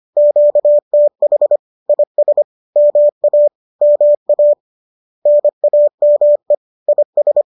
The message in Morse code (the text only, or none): QTH IS MA MA NAME IS